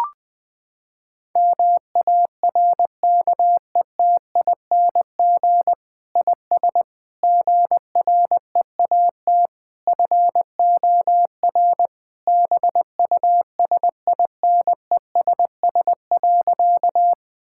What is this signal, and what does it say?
Morse code, heard as a keyed tone, about 21 wpm, 700 Hz. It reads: MARKETING IS GREAT FOR BUSINESS.
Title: Sentences from Top 1000 Words - Review Part 5 - 20wpm